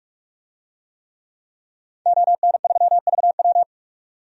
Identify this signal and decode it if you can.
Morse code, heard as a keyed tone, about 46 wpm, 700 Hz. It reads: ON3VW
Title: DXCC-100 Call Signs - 5 Characters - 45wpm